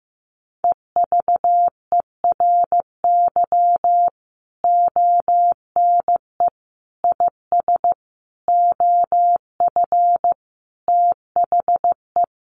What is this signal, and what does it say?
Morse code, heard as a keyed tone, about 15 wpm, 700 Hz. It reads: EVERY ONE IS OF THE
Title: Sentences from Top 1200 Words 15wpm